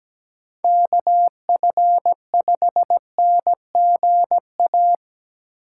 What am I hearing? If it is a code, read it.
Morse code, heard as a keyed tone, about 20 wpm, 700 Hz. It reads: KF5NGA
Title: Speed Racing - USA 2x3 Call Signs 17wpm